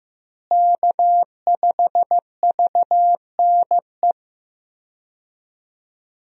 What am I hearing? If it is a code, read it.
Morse code, heard as a keyed tone, about 16 wpm, 700 Hz. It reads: K5VNE
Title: USA Call Signs - 5 Characters - Repeated 2x - 15wpm